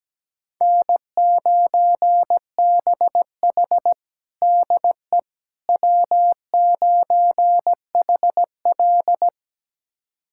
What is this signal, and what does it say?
Morse code, heard as a keyed tone, about 17 wpm, 700 Hz. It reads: N9BH DE W9HL
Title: USA Callsign DE Callsign - 4 Characters - 17wpm